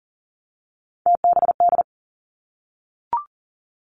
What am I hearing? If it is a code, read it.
Morse code, heard as a keyed tone, about 40 wpm, 700 Hz. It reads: TBD
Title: Three Challenging Letter-Number Combinations 40wpm